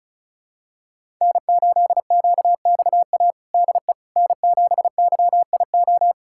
Morse code, heard as a keyed tone, about 36 wpm, 700 Hz. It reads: N8QXA DE N7YIO
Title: USA Callsign DE Callsign - 5 Characters - 35wpm